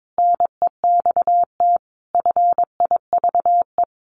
Morse code, heard as a keyed tone, about 22 wpm, 700 Hz. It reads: NEXT FIVE
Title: Sets of 5 Words 22wpm